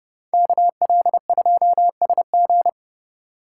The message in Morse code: KL2SG

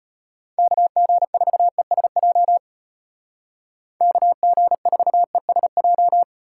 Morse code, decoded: KG4ESJ KG4ESJ

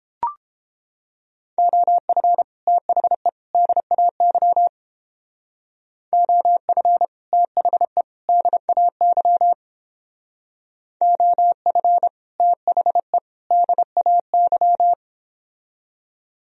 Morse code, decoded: OF THE DAY OF THE DAY OF THE DAY

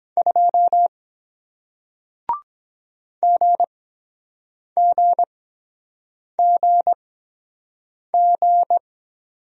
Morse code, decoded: 2 GGGG

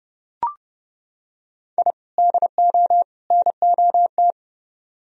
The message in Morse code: I DO NOT

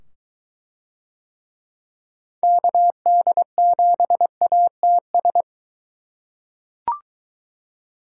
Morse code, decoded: KD7ATS

